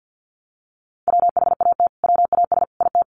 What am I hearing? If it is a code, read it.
Morse code, heard as a keyed tone, about 50 wpm, 700 Hz. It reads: WHAT WAS IT
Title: Sentences from Top 200 Words 50wpm